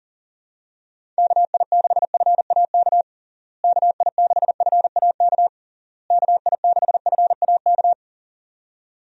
Morse code, heard as a keyed tone, about 37 wpm, 700 Hz. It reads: KI6FAK KI6FAK KI6FAK
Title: USA Call Signs - 6 Characters - Repeated 3x - 40wpm